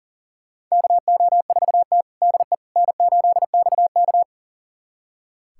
Morse code, heard as a keyed tone, about 40 wpm, 700 Hz. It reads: KO4T DE N8XK
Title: USA Callsign DE Callsign - 4 Characters - 40wpm